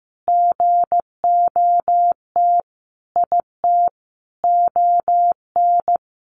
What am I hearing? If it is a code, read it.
Morse code, heard as a keyed tone, about 15 wpm, 700 Hz. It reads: GOT IT ON
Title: Sentences from Top 1400 Words 15wpm